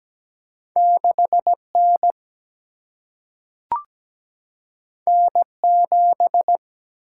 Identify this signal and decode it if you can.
Morse code, heard as a keyed tone, about 17 wpm, 700 Hz. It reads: 6N N7
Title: Single-Letter Single-Number 17wpm